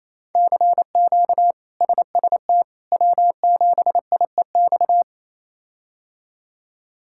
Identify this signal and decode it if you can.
Morse code, heard as a keyed tone, about 28 wpm, 700 Hz. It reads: CQ SST W7IEX
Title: K1USN SST QSO Practice - 28wpm